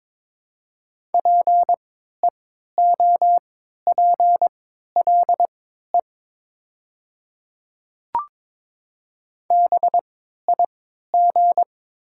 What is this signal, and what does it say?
Morse code, heard as a keyed tone, about 22 wpm, spaced as for 13 wpm, 700 Hz. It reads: PEOPLE BIG
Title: 3x Character Spacing - Top 200 Words - 22wpm